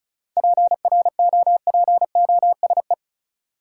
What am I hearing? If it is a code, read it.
Morse code, heard as a keyed tone, about 34 wpm, 700 Hz. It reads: PROPOSE
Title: Seven Letter Words - 35wpm